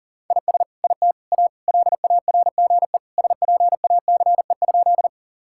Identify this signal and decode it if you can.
Morse code, heard as a keyed tone, about 37 wpm, 700 Hz. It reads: IS IT A LARGE SPACE?
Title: Sentences from Top 1000 Words - Review Part 5 - 40wpm